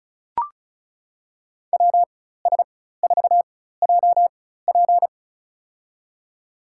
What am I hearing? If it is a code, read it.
Morse code, heard as a keyed tone, about 34 wpm, spaced as for 16 wpm, 700 Hz. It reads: WS4JP
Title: USA Call Signs - 5 Characters - 4x Character Spacing - 35wpm